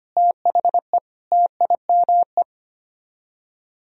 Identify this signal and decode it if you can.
Morse code, heard as a keyed tone, about 32 wpm, 700 Hz. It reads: THE TIME